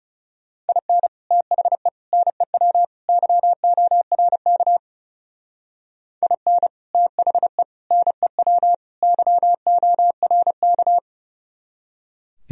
Morse code, heard as a keyed tone, about 38 wpm, 700 Hz. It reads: IN THE NEW YORK IN THE NEW YORK